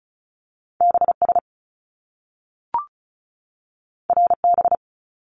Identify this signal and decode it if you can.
Morse code, heard as a keyed tone, about 36 wpm, 700 Hz. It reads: BS RB
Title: Top Two Letter Combinations in Words 35wpm